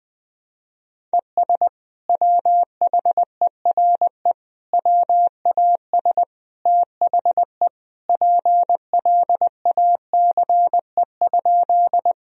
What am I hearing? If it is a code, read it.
Morse code, heard as a keyed tone, about 20 wpm, 700 Hz. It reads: ES WHERE WAS THE PLACE?
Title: Sentences from Top 200 Words and Abbreviations 20wpm